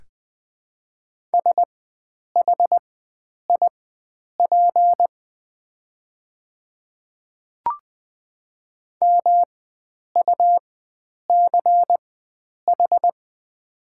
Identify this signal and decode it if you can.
Morse code, heard as a keyed tone, about 20 wpm, spaced as for 9 wpm, 690 Hz. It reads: SHIP MUCH
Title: 4x Character Spacing - Top 500 Words - 20wpm